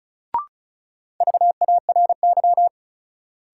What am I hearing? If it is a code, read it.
Morse code, heard as a keyed tone, about 36 wpm, 700 Hz. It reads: VARY
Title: Sentences from Top 1200 Words - New Word Drill Alt - 35wpm